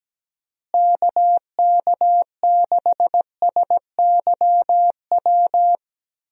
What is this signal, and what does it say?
Morse code, heard as a keyed tone, about 17 wpm, 700 Hz. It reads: KK6SYW
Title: USA Call Signs - 6 Characters - Repeated 3x - 17wpm